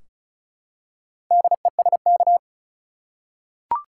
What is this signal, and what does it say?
Morse code, heard as a keyed tone, about 36 wpm, 700 Hz. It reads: DESK
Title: Four Letter Words - 35wpm